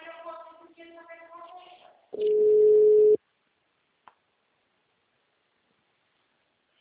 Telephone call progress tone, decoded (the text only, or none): ringback tone